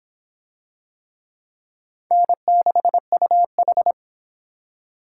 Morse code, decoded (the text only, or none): N6UH